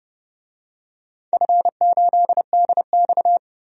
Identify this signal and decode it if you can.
Morse code, heard as a keyed tone, about 32 wpm, 700 Hz. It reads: F8DX
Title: DXCC-100 Call Signs - 4 Characters - Repeated 2x - 30wpm